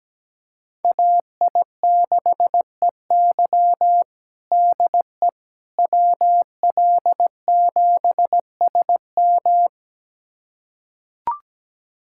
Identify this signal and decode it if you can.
Morse code, heard as a keyed tone, about 17 wpm, 700 Hz. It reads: AI6EY DE WL7SM
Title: USA Callsign DE Callsign - 5 Characters - 17wpm